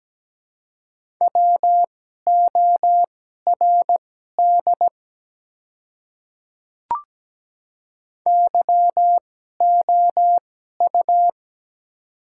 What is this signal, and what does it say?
Morse code, heard as a keyed tone, about 17 wpm, spaced as for 12 wpm, 700 Hz. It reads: WORD YOU
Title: 2x Character Spacing - Top 100 Words - 17wpm